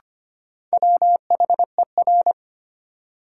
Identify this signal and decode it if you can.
Morse code, heard as a keyed tone, about 25 wpm, 700 Hz. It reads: WHER
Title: Sentences from Top 1500 Words - New Word Drill Alt - 25wpm